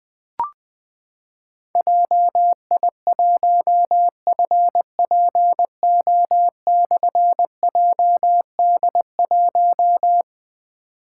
Morse code, heard as a keyed tone, about 20 wpm, 700 Hz. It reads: JI1FPO/JD1
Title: Difficult Call Signs 20wpm